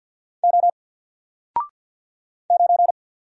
Morse code, heard as a keyed tone, about 50 wpm, 700 Hz. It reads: O 9
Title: Single Letter-Number 50wpm